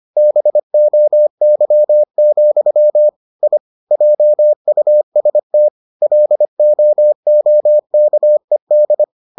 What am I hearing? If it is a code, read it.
Morse code, heard as a keyed tone, about 25 wpm, 600 Hz. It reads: BOY, I JUST LOOKED